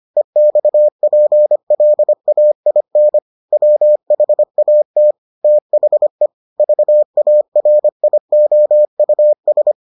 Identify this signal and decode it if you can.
Morse code, heard as a keyed tone, about 25 wpm, 600 Hz. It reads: EXPLAIN WHAT THE VARIOUS